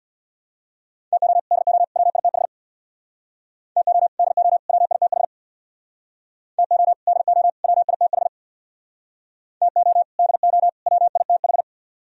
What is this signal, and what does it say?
Morse code, heard as a keyed tone, about 66 wpm, 700 Hz. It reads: TO DO WITH TO DO WITH TO DO WITH TO DO WITH